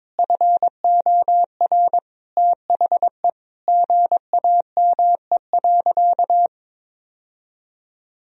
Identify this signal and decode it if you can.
Morse code, heard as a keyed tone, about 22 wpm, 700 Hz. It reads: FOR THE GAME.
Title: Sentences from Top 1000 Words - Review Part 4 - 22wpm